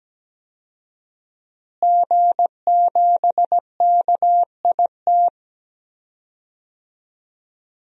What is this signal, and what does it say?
Morse code, heard as a keyed tone, about 17 wpm, 700 Hz. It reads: G7KIT
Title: DXCC-100 Call Signs - 5 Characters - Repeated 2x - 17wpm